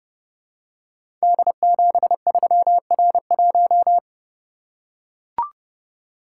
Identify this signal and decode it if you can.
Morse code, heard as a keyed tone, about 31 wpm, 700 Hz. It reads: D73R1
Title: Code Groups - 5 Letter:Numbers - 30wpm